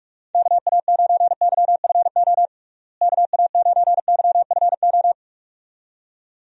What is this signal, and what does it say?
Morse code, heard as a keyed tone, about 42 wpm, 700 Hz. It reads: KA9YFO KA9YFO
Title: USA Call Signs - 6 Characters - Repeated 2x - 45wpm